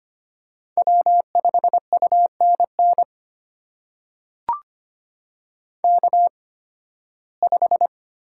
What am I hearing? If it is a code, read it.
Morse code, heard as a keyed tone, about 25 wpm, spaced as for 16 wpm, 700 Hz. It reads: W5UNN K5
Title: USA Call Signs - 5 Characters - 8x Character Spacing - 25wpm